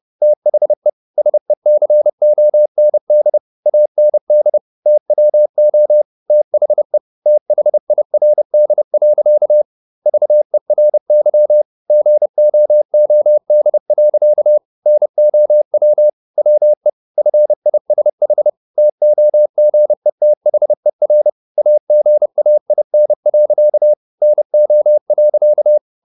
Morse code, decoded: THE SECOND AND TWO THE THIRD. VERY GOOD. NOW WE FISH TOGETHER AGAIN. NO.

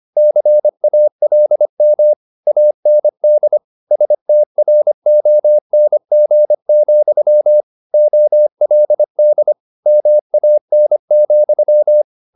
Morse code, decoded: CALM AND STRONG, OLD MAN,

